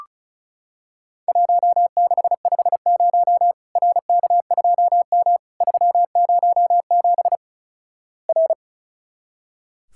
Morse code, heard as a keyed tone, about 36 wpm, 700 Hz. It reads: 1650 RK2M 307 R
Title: Worked All Europe DX Contest - QTC Exchange - 35wpm